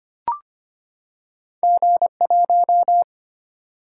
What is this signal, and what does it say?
Morse code, heard as a keyed tone, about 25 wpm, 700 Hz. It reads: G 1
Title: Single-Letter Single-Number 25wpm